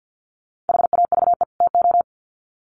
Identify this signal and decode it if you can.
Morse code, heard as a keyed tone, about 50 wpm, 700 Hz. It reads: HAVE TO
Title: Sets of 2 Words 50wpm